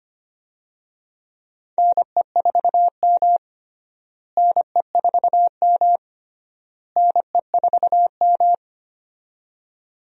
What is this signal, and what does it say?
Morse code, heard as a keyed tone, about 25 wpm, 700 Hz. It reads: NE4M NE4M NE4M
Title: USA Call Signs - 4 Characters - Repeated 3x - 25wpm